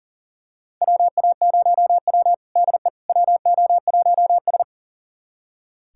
Morse code, heard as a keyed tone, about 38 wpm, 700 Hz. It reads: WA0W DE WO1S
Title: USA Callsign DE Callsign - 4 Characters - 40wpm